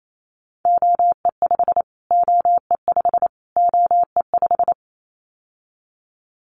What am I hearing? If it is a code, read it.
Morse code, heard as a keyed tone, about 28 wpm, 700 Hz. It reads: OE5 OE5 OE5
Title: Callsign Prefixes for Top 25 DXCC - Repeated 3x - 28wpm